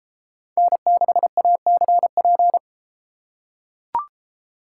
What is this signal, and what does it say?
Morse code, heard as a keyed tone, about 32 wpm, 700 Hz. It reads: N6ACP